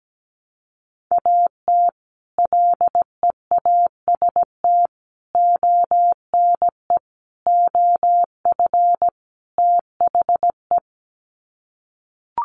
Morse code, heard as a keyed tone, about 17 wpm, 700 Hz. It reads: AT LEAST ONE OF THE
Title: Sets of 5 Words Encore 17wpm